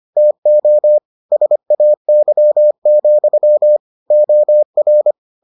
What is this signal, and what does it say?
Morse code, heard as a keyed tone, about 25 wpm, 600 Hz. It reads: TO SAY, OR